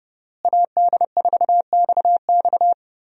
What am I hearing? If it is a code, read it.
Morse code, heard as a keyed tone, about 30 wpm, 700 Hz. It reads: AD4XX
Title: USA Call Signs - 5 Characters - Repeated 3x - 30wpm